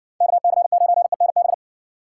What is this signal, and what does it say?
Morse code, heard as a keyed tone, about 55 wpm, 700 Hz. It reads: KC9ENB